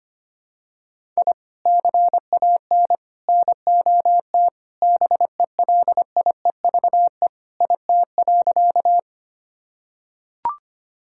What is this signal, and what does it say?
Morse code, heard as a keyed tone, about 25 wpm, 700 Hz. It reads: I CAN NOT BELIEVE IT.
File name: practice / 25wpm / Sentences with 4-Word Phrases - 25wpm